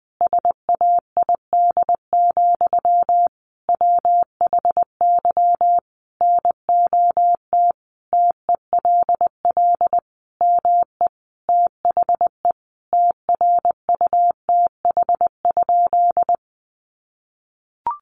Morse code, heard as a keyed tone, about 19 wpm, 700 Hz. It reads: SAID, WHY NOT TELL ME THE TRUTH?
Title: Sentences from Top 800 Words 20wpm